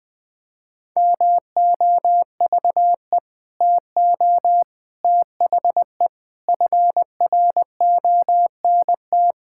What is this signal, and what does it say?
Morse code, heard as a keyed tone, about 20 wpm, 700 Hz. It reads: MOVE TO THE FRONT
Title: Sentences with 5-Word Phrases - 20wpm